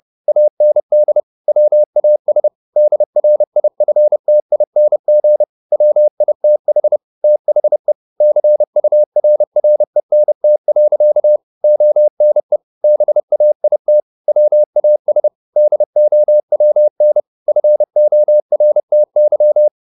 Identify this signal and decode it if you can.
Morse code, heard as a keyed tone, about 30 wpm, 600 Hz. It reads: AND WAS DRIFTING WITH THE CURRENT. ONE BAIT WAS DOWN FORTY
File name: the old man and the sea / the old man and the sea 1 30wpm